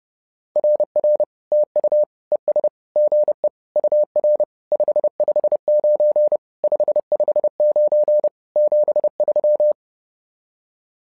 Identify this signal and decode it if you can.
Morse code, heard as a keyed tone, about 30 wpm, 600 Hz. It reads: RR TU ES GE UR 559 559 73
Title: SOTA QSO Practice 30wpm